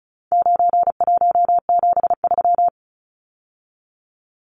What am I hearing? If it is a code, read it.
Morse code, heard as a keyed tone, about 34 wpm, 700 Hz. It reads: 9173
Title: Numbers - 4-Digit 35wpm